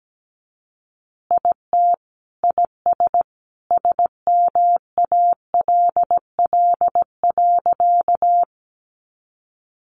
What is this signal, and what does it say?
Morse code, heard as a keyed tone, about 17 wpm, 700 Hz. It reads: IT IS SMALL.
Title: Easy Phrases 17wpm